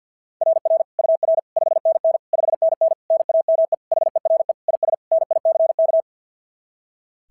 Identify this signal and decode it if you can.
Morse code, heard as a keyed tone, about 46 wpm, 660 Hz. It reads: RR UR 5NN 5NN NAME HERE IS NICK
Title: SOTA QSO Practice 50wpm